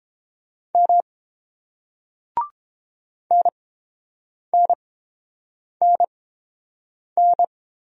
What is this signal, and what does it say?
Morse code, heard as a keyed tone, about 30 wpm, spaced as for 7 wpm, 700 Hz. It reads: M NNNN